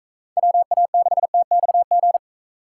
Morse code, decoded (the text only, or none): WA6TXG